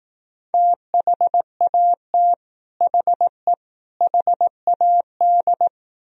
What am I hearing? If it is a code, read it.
Morse code, heard as a keyed tone, about 18 wpm, 700 Hz. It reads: THAT HE HAD
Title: Speed-Racing - Sets of 3 Words 15wpm